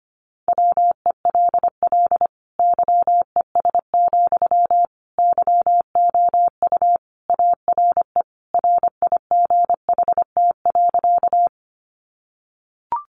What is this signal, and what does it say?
Morse code, heard as a keyed tone, about 25 wpm, 700 Hz. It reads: WELL YES, YOU ARE RIGHT.
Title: Sentences from Top 500 Words 25wpm